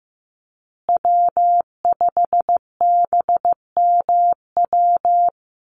Morse code, heard as a keyed tone, about 15 wpm, 700 Hz. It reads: W5BMW
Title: USA Call Signs Repeated 2x 15wpm